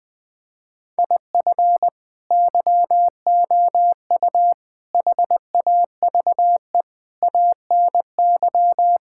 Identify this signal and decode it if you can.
Morse code, heard as a keyed tone, about 20 wpm, 700 Hz. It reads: IF YOU HAVE ANY
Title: Sets of 8 Words - 20wpm